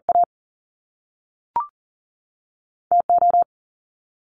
Morse code, heard as a keyed tone, about 42 wpm, 700 Hz. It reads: A TO
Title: Top Two Letter Combinations in Words 40wpm